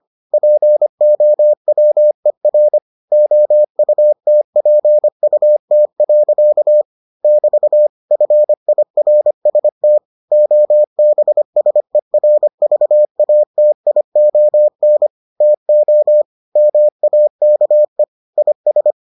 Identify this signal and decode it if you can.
Morse code, heard as a keyed tone, about 25 wpm, 600 Hz. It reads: POWER OUTPUT. <BT> FIRST OBSERVATION TO MAKE IS